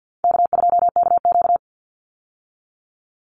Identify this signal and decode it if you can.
Morse code, heard as a keyed tone, about 50 wpm, 700 Hz. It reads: K2KQ
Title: Call Signs Repeated 3x - 50wpm